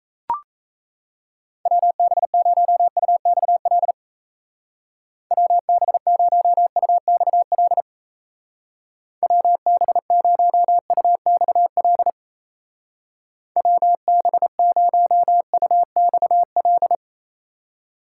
Morse code, heard as a keyed tone, about 37 wpm, 700 Hz. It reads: WB0UXL WB0UXL WB0UXL WB0UXL